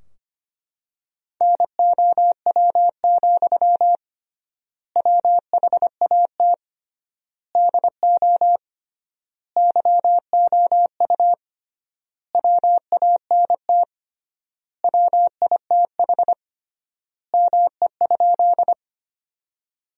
Morse code, heard as a keyed tone, about 25 wpm, 700 Hz. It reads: NOW, WHAT DO YOU WANT WITH ME?
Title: 3x Word-Spacing - Sentences from Top 200 Words - 25wpm